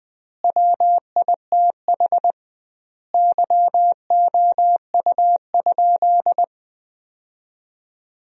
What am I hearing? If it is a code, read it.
Morse code, heard as a keyed tone, about 21 wpm, 700 Hz. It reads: WITH YOU?
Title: 2x Word-Spacing - Sentences from Top 500 Words - 20wpm